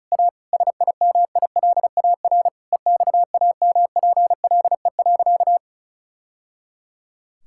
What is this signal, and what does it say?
Morse code, heard as a keyed tone, about 36 wpm, 700 Hz. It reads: A SIMILAR EXAMPLE.
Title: Sentences from Top 1000 Words - Review Part 5 - 35wpm